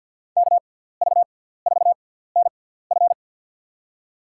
Morse code, heard as a keyed tone, about 52 wpm, spaced as for 17 wpm, 700 Hz. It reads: KV4NF
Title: USA Call Signs - 5 Characters - 6x Character Spacing - 50wpm